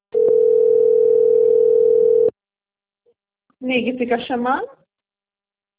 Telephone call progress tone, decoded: ringback tone

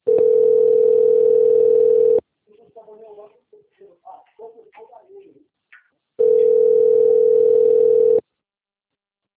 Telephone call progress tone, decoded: ringback tone

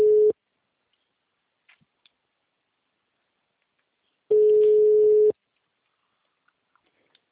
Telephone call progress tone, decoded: ringback tone

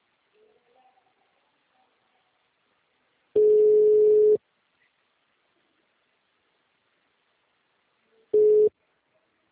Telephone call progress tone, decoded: ringback tone

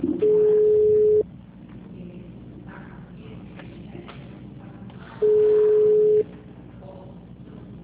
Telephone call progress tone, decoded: ringback tone

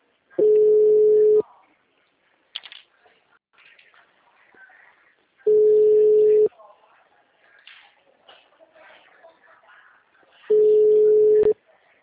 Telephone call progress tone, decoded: ringback tone

